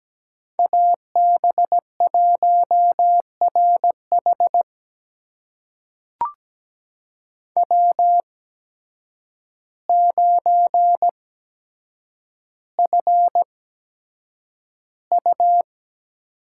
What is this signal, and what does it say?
Morse code, heard as a keyed tone, about 17 wpm, spaced as for 8 wpm, 700 Hz. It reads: AB1RH W9FU
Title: USA Call Signs - 5 Characters - 8x Character Spacing - 17wpm